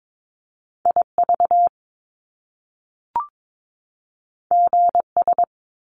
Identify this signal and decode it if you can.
Morse code, heard as a keyed tone, about 22 wpm, 700 Hz. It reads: IV GS
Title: Top Two Letter Combinations in Words 22wpm